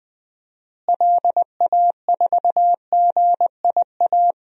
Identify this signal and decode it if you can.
Morse code, heard as a keyed tone, about 20 wpm, 700 Hz. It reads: LA4GIA
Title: DXCC-100 Call Signs - 6 Characters - Repeated 2x - 20wpm